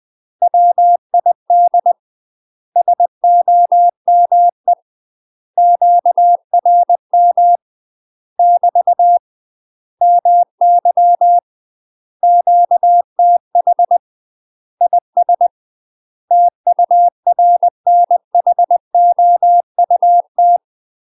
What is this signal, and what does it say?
Morse code, heard as a keyed tone, about 20 wpm, 700 Hz. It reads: WID SOME QRM <BT> MY QTH IS TURNHOUT
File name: UK MASTER QSOs TRAINING / 20WPM / UK MASTER QSOs 0018